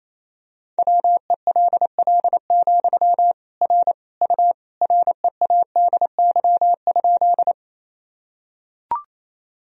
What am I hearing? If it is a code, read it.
Morse code, heard as a keyed tone, about 28 wpm, 700 Hz. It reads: WELL, R U READY?
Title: Sentences from Top 400 Words and Abbreviations 28wpm